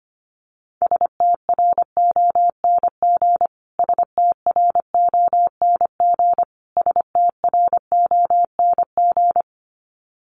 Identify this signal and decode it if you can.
Morse code, heard as a keyed tone, about 25 wpm, 700 Hz. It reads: STRONG STRONG STRONG
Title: Top 500 Words Repeated 25wpm